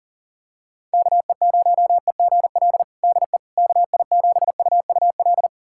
Morse code, heard as a keyed tone, about 37 wpm, 700 Hz. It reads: KE0EGL DE KI7UUL
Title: USA Callsign DE Callsign 40wpm